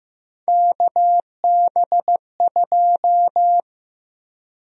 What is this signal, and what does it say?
Morse code, heard as a keyed tone, about 15 wpm, 700 Hz. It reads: KB2
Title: USA Call Sign Prefixes - 15wpm